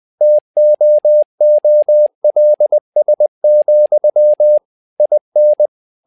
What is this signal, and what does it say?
Morse code, heard as a keyed tone, about 20 wpm, 600 Hz. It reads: TOOLS, IN
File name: tower trouble 20wpm